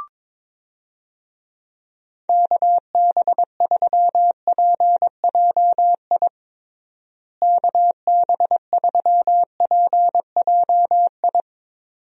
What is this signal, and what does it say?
Morse code, heard as a keyed tone, about 22 wpm, 700 Hz. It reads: KB3PJI KB3PJI
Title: USA Call Signs - 6 Characters - Repeated 2x - 22wpm